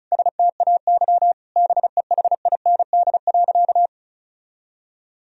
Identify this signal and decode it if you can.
Morse code, heard as a keyed tone, about 36 wpm, 700 Hz. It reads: STAY BEHIND.
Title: Sentences from Top 1000 Words - Review Part 1 - 35wpm